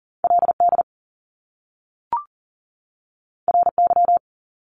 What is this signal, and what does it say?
Morse code, heard as a keyed tone, about 40 wpm, 700 Hz. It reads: LD RY
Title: Top Two Letter Combinations in Words 40wpm